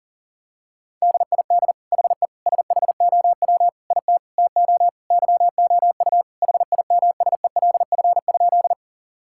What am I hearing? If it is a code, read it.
Morse code, heard as a keyed tone, about 40 wpm, 700 Hz. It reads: DID HE SHOW IT TO YOU HIMSELF?
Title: Sentences from Top 1600 Words 40wpm